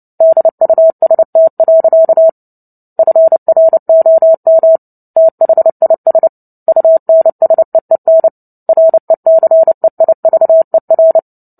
Morse code, heard as a keyed tone, about 30 wpm, 650 Hz. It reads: DUST. FROM THIS UNSEEN RECEIVER
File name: Book Two / 03-Chapter 3 - 29 WPM